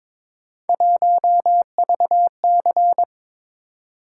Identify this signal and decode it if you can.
Morse code, heard as a keyed tone, about 22 wpm, 700 Hz. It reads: 1VC
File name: practice / 22wpm / Three Letter-Number Combinations - 22wpm